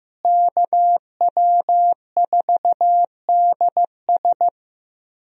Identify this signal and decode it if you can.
Morse code, heard as a keyed tone, about 15 wpm, 700 Hz. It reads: KW4DS
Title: USA Call Signs - 5 Characters - Repeated 2x - 15wpm